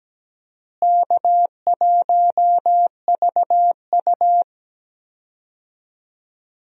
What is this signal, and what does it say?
Morse code, heard as a keyed tone, about 17 wpm, 700 Hz. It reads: K1VU
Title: USA Call Signs - 4 Characters - Repeated 3x - 17wpm